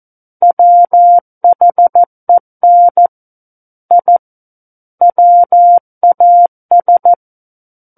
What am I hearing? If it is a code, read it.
Morse code, heard as a keyed tone, about 14 wpm, 700 Hz. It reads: WHEN I WAS
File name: Book One / 05-Chapter 5 - 14 WPM